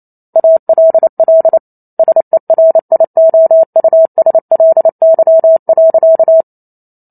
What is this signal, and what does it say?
Morse code, heard as a keyed tone, about 28 wpm, 650 Hz. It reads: ALL SERIOUSLY.